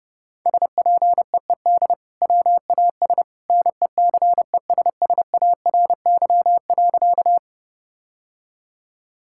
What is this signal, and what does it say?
Morse code, heard as a keyed tone, about 30 wpm, 700 Hz. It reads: SPEED WAS NECESSARY.
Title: Sentences from Top 1000 Words - Review Part 5 - 30wpm